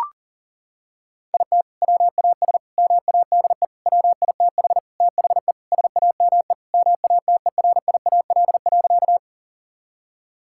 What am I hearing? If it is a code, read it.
Morse code, heard as a keyed tone, about 40 wpm, 700 Hz. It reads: IT WAS MADE WITH THE SAME MATERIAL.
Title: Sentences from Top 1000 Words - Review Part 5 - 40wpm